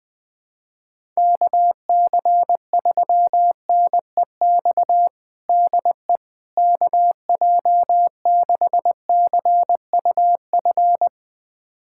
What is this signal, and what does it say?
Morse code, heard as a keyed tone, about 20 wpm, 700 Hz. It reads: KC3NEX DE KJ6CUF
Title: USA Callsign DE Callsign - 6 Characters - 20wpm